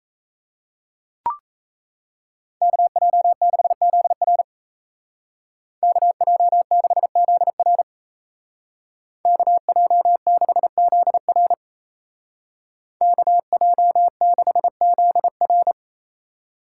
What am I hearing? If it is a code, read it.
Morse code, heard as a keyed tone, about 34 wpm, 700 Hz. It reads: KJ6ZR KJ6ZR KJ6ZR KJ6ZR
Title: Speed Racing - USA Call Signs 28wpm